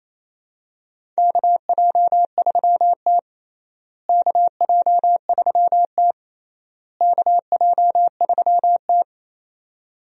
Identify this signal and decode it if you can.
Morse code, heard as a keyed tone, about 28 wpm, 700 Hz. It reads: KJ3T KJ3T KJ3T
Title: USA Call Signs - 4 Characters - Repeated 3x - 28wpm